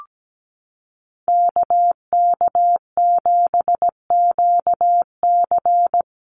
Morse code, heard as a keyed tone, about 17 wpm, 700 Hz. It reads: KK7QC
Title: USA Call Signs Repeated 2x 17wpm